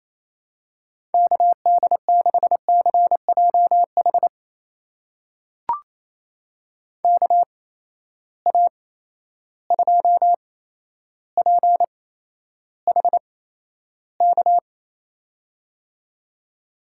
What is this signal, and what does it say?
Morse code, heard as a keyed tone, about 28 wpm, spaced as for 12 wpm, 700 Hz. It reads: KD6CJH KA2PHK
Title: USA Call Signs - 6 Characters - 8x Character Spacing - 28wpm